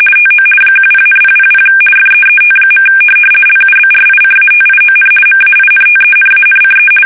RTTY, about 50 baud. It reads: TO 10,000 FEET. LUBRICATION MAINTENANCE
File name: fsk850-50B / teletype-corp